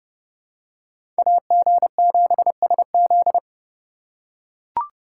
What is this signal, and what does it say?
Morse code, heard as a keyed tone, about 30 wpm, 700 Hz. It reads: AG7SZ
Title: USA Call Signs - 5 Characters - 2x Character Spacing - 30wpm